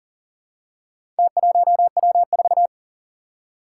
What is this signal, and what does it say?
Morse code, heard as a keyed tone, about 39 wpm, 700 Hz. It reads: T1W4
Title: Four Challenging Letter-Number Combinations 40wpm